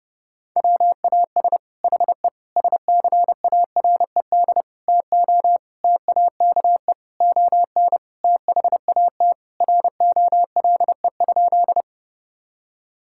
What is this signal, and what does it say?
Morse code, heard as a keyed tone, about 30 wpm, 700 Hz. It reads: WAS HE SCARED TO TAKE ON THAT ROLE?
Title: Sentences from Top 1000 Words - Review Part 6 - 30wpm